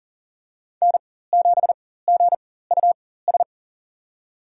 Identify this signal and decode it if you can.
Morse code, heard as a keyed tone, about 40 wpm, spaced as for 19 wpm, 700 Hz. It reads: N7GUS
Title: USA Call Signs - 4x Character Spacing - 40wpm